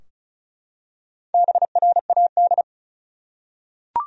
Morse code, heard as a keyed tone, about 36 wpm, 700 Hz. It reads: BRAD
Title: USA Common Names - Max 5 Letters - 35wpm